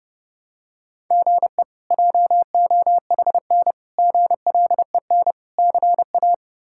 Morse code, heard as a keyed tone, about 30 wpm, 700 Hz. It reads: GE JOHN GLEN CA
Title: K1USN SST QSO Practice - 30wpm